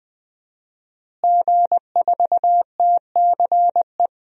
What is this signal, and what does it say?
Morse code, heard as a keyed tone, about 20 wpm, 700 Hz. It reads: G4TCE
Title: DXCC-100 Call Signs - 5 Characters - Repeated 3x - 20wpm